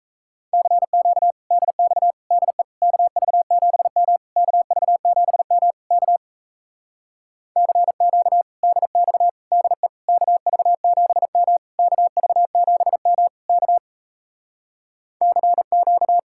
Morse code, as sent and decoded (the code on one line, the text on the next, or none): -.-. --.- / -.. -..- / -.. / -.- ...- --... -- / -.- ...- --... -- / -.- / -.-. --.- / -.. -..- / -.. / -.- ...- --... -- / -.- ...- --... -- / -.- / -.-. --.-
CQ DX DE KV7M KV7M K CQ DX DE KV7M KV7M K CQ